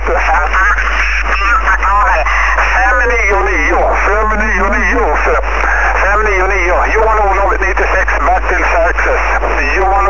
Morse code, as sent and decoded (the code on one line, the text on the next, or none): none